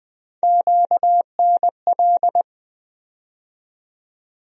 Morse code, heard as a keyed tone, about 21 wpm, 700 Hz. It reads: QNL
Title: USA Call Sign Postfixes - 3 Letters 20wpm